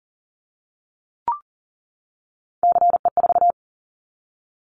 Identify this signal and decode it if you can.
Morse code, heard as a keyed tone, about 41 wpm, 700 Hz. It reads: CE4
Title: Callsign Prefixes for Top 100 DXCC - 40wpm